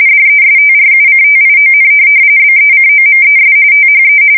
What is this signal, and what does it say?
RTTY, about 45.45 baud. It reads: DESTRUCTION BECOMES THE
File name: fsk170 / USS-New-Jersey